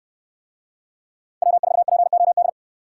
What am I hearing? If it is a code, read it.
Morse code, heard as a keyed tone, about 64 wpm, 700 Hz. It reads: W4POD